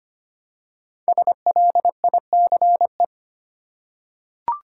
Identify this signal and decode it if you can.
Morse code, heard as a keyed tone, about 25 wpm, 700 Hz. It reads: SLICE
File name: 5 Letter Words - First 3 Letters Given - 25wpm